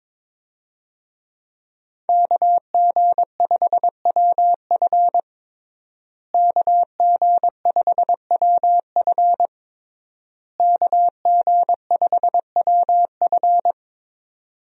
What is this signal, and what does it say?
Morse code, heard as a keyed tone, about 22 wpm, 700 Hz. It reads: KG5WF KG5WF KG5WF
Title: USA Call Signs - 5 Characters - Repeated 3x - 22wpm